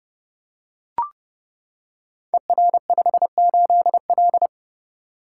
Morse code, heard as a keyed tone, about 30 wpm, 700 Hz. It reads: ER58L
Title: Code Groups - 5 Letter:Numbers - 30wpm